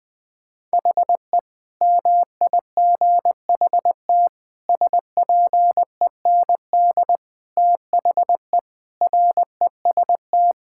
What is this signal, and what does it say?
Morse code, heard as a keyed tone, about 21 wpm, 700 Hz. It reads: HE MIGHT SPEND THE REST
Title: Sentences with 8-Word Phrases - 20wpm